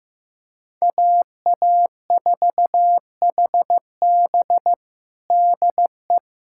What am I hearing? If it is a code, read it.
Morse code, heard as a keyed tone, about 15 wpm, 700 Hz. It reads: AA4HB DE
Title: USA Callsign DE Callsign - 5 Characters - 15wpm